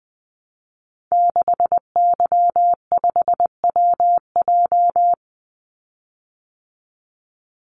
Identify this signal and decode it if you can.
Morse code, heard as a keyed tone, about 20 wpm, 700 Hz. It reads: 6Y5WJ
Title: Call Signs Repeated 3x - 20wpm